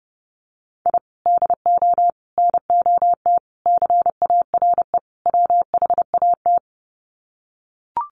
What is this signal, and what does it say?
Morse code, heard as a keyed tone, about 28 wpm, 700 Hz. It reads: I DO NOT CARE WHAT
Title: Sets of 5 Words 30wpm